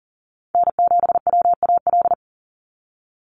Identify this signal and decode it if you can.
Morse code, heard as a keyed tone, about 40 wpm, 690 Hz. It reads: N7WAL
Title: USA Call Signs 40wpm